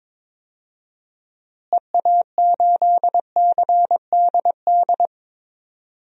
Morse code, heard as a keyed tone, about 22 wpm, 700 Hz. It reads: EA8CDD
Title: DXCC-100 Call Signs - 6 Characters - Repeated 3x - 22wpm